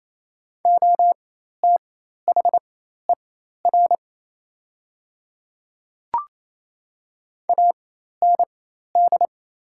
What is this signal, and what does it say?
Morse code, heard as a keyed tone, about 28 wpm, spaced as for 13 wpm, 700 Hz. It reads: OTHER AND
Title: 4x Character Spacing - Top 100 Words - 28wpm